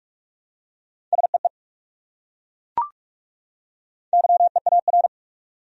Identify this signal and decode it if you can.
Morse code, heard as a keyed tone, about 40 wpm, 700 Hz. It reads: SEE YEAR